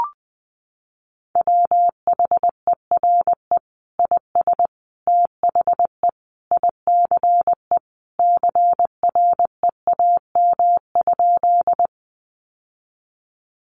Morse code, heard as a keyed tone, about 20 wpm, 700 Hz. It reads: WHERE IS THE ICE CREAM?
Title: Sentences from Top 1500 Words 20wpm